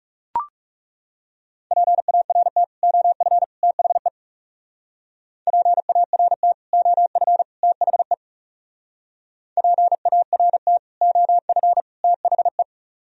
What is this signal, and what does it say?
Morse code, heard as a keyed tone, about 37 wpm, 700 Hz. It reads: PART OF THE PART OF THE PART OF THE
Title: Speed-Racing - Sets of 3 Words 30wpm